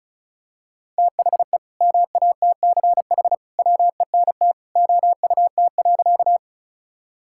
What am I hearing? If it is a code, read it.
Morse code, heard as a keyed tone, about 34 wpm, 700 Hz. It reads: THE MATCH WENT OUT.
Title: Sentences from Top 1600 Words 35wpm